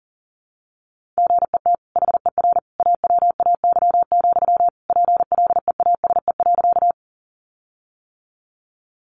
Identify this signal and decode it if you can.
Morse code, heard as a keyed tone, about 40 wpm, 700 Hz. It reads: GET HER AWAY, PLEASE.
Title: Sentences from Top 1500 Words 40wpm